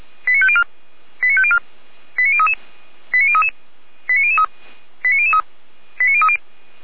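Five-tone selective call (ZVEI), heard as quick runs of five tones, tones of about 70 ms. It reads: 89504 89504 89033 89033 89003 89003 89030
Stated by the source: ZVEI five-tone selective call 89070 received by SDR.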